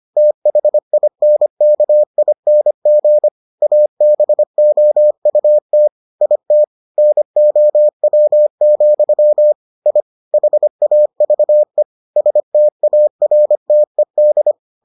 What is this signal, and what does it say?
Morse code, heard as a keyed tone, about 25 wpm, 600 Hz. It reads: THINKING ABOUT IT NOW, I HAVE STARTED